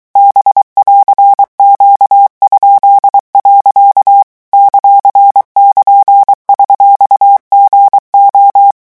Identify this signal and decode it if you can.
Morse code, heard as a keyed tone, about 23 wpm, 800 Hz. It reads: B<AR>Q?. ;<KN>$GO
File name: morse mp3 / revision19-vitesse 22 mots